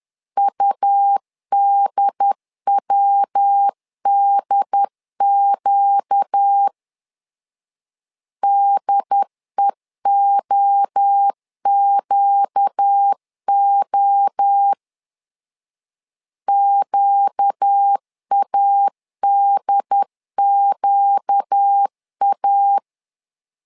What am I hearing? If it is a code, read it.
Morse code, heard as a keyed tone, about 10 wpm, 800 Hz. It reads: UDWDQ DEOQO QADQA
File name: LEZIONI / 4 Q 10wpm 2210